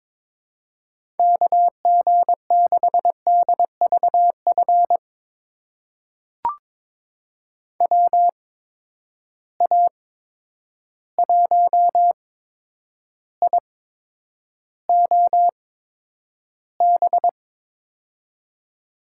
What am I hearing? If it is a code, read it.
Morse code, heard as a keyed tone, about 22 wpm, spaced as for 9 wpm, 700 Hz. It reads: KG6DVF WA1IOB
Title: USA Call Signs - 6 Characters - 8x Character Spacing - 22wpm